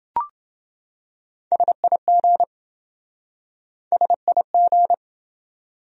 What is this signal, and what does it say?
Morse code, heard as a keyed tone, about 28 wpm, 700 Hz. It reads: SIG SIG